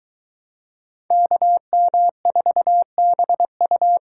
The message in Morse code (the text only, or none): KM4BU